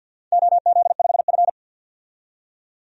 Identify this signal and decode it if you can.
Morse code, heard as a keyed tone, about 51 wpm, 700 Hz. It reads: OZ5F
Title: DXCC-100 Call Signs - 4 Characters - 50wpm